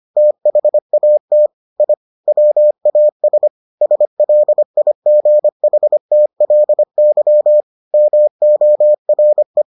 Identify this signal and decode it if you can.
Morse code, heard as a keyed tone, about 25 wpm, 600 Hz. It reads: THAT I WAS SLIGHTLY MORE